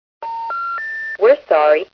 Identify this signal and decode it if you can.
Special information tone, with speech.